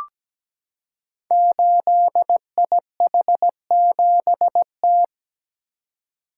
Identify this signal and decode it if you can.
Morse code, heard as a keyed tone, about 17 wpm, 700 Hz. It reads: 8IH7T
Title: Code Groups - 5 Letter:Numbers - 17wpm